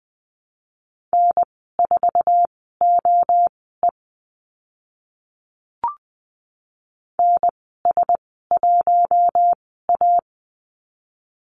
Morse code, heard as a keyed tone, about 20 wpm, spaced as for 14 wpm, 700 Hz. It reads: N4OE NS1A
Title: 048 Lesson 9g - Number 4 - Call Signs 20wpm